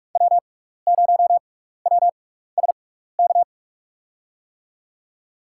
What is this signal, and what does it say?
Morse code, heard as a keyed tone, about 45 wpm, spaced as for 15 wpm, 690 Hz. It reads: W0WSK
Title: USA Call Signs - 5 Characters - 6x Character Spacing - 45wpm